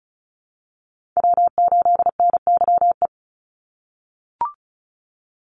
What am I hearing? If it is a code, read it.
Morse code, heard as a keyed tone, about 34 wpm, 700 Hz. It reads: W8NYE
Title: USA Call Signs 35wpm